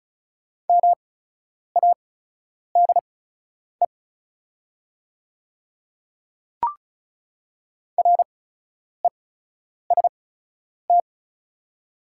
Morse code, heard as a keyed tone, about 36 wpm, spaced as for 10 wpm, 690 Hz. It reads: MADE REST
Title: 8x Character Spacing - Top 500 Words - 35wpm